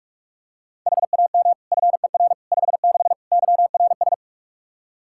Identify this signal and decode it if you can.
Morse code, heard as a keyed tone, about 44 wpm, 700 Hz. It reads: HAM FER 56 YRS
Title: Practice QSO - Exchange 3 - More Info - 45wpm